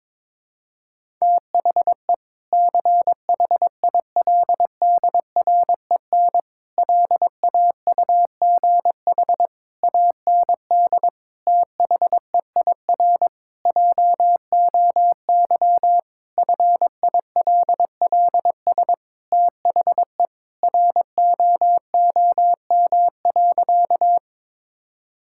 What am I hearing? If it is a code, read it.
Morse code, heard as a keyed tone, about 22 wpm, 700 Hz. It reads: THE CHILDREN LAUGH AND THEIR JOY FILLS THE ROOM.
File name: practice / 22wpm / Sentences with 2-Word Phrases - 22wpm